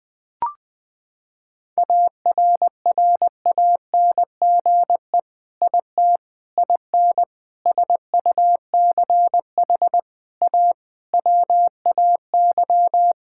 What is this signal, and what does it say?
Morse code, heard as a keyed tone, about 20 wpm, 700 Hz. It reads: ARRANGE IT IN SUCH A WAY
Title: Sentences with 5-Word Phrases - 20wpm